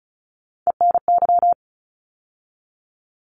Morse code, heard as a keyed tone, about 34 wpm, 700 Hz. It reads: ENY
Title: ARRL Sections 35wpm